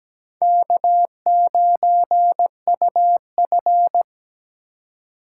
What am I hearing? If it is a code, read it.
Morse code, heard as a keyed tone, about 17 wpm, 700 Hz. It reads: K9UF
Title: USA Call Signs - 3x Character Spacing - 17wpm